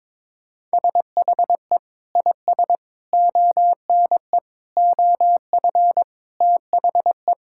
Morse code, heard as a keyed tone, about 21 wpm, 700 Hz. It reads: SHE IS ONE OF THE
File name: practice / 22wpm / Sentences with 2-Word Phrases - 22wpm